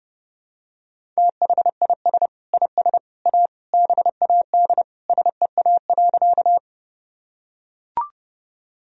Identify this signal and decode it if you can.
Morse code, heard as a keyed tone, about 30 wpm, 700 Hz. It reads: THIS IS A BAD SEA.
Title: Sentences from Top 1000 Words - Review Part 1 - 30wpm